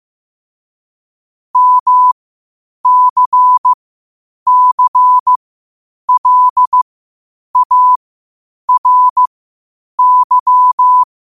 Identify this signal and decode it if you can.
Morse code, heard as a keyed tone, about 15 wpm, spaced as for 8 wpm, 1000 Hz. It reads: MCCLARY
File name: qotdfiles / 8wpm